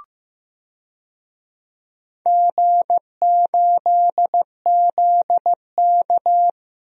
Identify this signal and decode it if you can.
Morse code, heard as a keyed tone, about 15 wpm, 700 Hz. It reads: G8ZK
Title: DXCC-100 Call Signs - 4 Characters - Repeated 2x - 15wpm